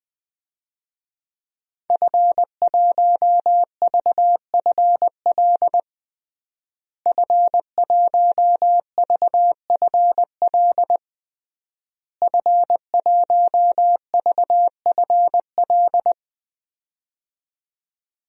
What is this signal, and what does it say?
Morse code, heard as a keyed tone, about 19 wpm, 700 Hz. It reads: F1VFL F1VFL F1VFL
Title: DXCC-100 Call Signs - 5 Characters - Repeated 3x - 20wpm